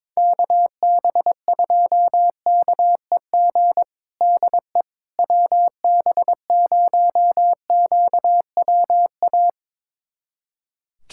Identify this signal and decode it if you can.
Morse code, heard as a keyed tone, about 22 wpm, 700 Hz. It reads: KB2KEG DE WB0QWA
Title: USA Callsign DE Callsign 22wpm